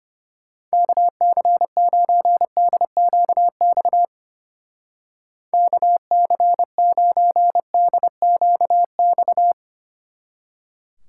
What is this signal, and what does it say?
Morse code, heard as a keyed tone, about 31 wpm, 700 Hz. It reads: KC9DQX KC9DQX
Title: Speed Racing - USA 2x3 Call Signs 25wpm